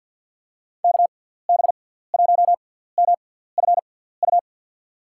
Morse code, heard as a keyed tone, about 52 wpm, spaced as for 17 wpm, 700 Hz. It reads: KB1MFU
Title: USA Call Signs - 6x Character Spacing - 50wpm